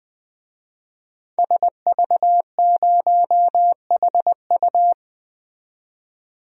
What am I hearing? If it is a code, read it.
Morse code, heard as a keyed tone, about 20 wpm, 700 Hz. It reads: SV0HU
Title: DXCC-100 Call Signs - 5 Characters - 20wpm